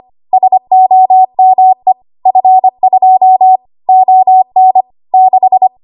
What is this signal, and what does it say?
Morse code, heard as a keyed tone, about 25 wpm, 750 Hz. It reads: SOME F2 ON 6
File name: Morse / 25WPM